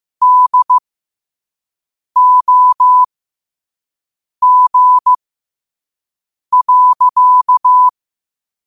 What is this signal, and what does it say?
Morse code, heard as a keyed tone, about 15 wpm, spaced as for 5 wpm, 1000 Hz. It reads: DOG.
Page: New quotes every day in morse code at 5 Words per minute.